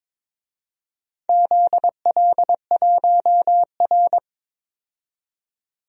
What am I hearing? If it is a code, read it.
Morse code, heard as a keyed tone, about 23 wpm, 700 Hz. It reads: ZL1R
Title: DXCC-100 Call Signs - 4 Characters - 22wpm